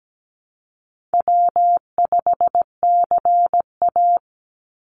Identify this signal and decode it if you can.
Morse code, heard as a keyed tone, about 17 wpm, 700 Hz. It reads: W5CA
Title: USA Call Signs 17wpm